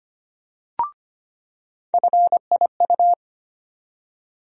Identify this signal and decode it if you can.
Morse code, heard as a keyed tone, about 25 wpm, 700 Hz. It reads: FIU
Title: USA Call Sign Postfixes - 3 Letters 25wpm